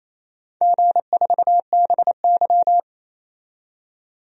Morse code, heard as a keyed tone, about 28 wpm, 690 Hz. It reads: G4BY
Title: DXCC-100 Call Signs - 4 Characters - 28wpm